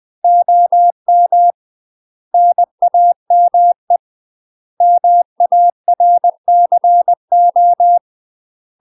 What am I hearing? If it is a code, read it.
Morse code, heard as a keyed tone, about 20 wpm, 700 Hz. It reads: OM NAME MARCO
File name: UK MASTER QSOs TRAINING/20WPM